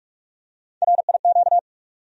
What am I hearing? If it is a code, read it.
Morse code, heard as a keyed tone, about 43 wpm, 700 Hz. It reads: RIQ
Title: USA Call Sign Postfixes - 3 Letters 45wpm